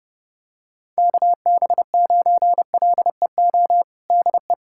Morse code, heard as a keyed tone, about 30 wpm, 700 Hz. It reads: KB9LEO DE
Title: USA Callsign DE Callsign - 6 Characters - 30wpm